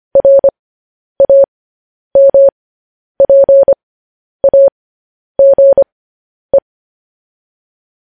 Morse code, heard as a keyed tone, about 25 wpm, spaced as for 10 wpm, 550 Hz. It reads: RAMPAGE
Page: News Headlines in Morse Code at 10 WPM